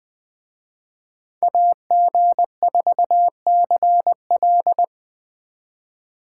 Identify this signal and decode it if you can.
Morse code, heard as a keyed tone, about 20 wpm, 700 Hz. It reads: AG4CL